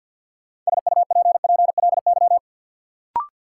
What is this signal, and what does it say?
Morse code, heard as a keyed tone, about 48 wpm, 700 Hz. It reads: SUPPLY
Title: Sentences from Top 1200 Words - New Word Drill Alt - 50wpm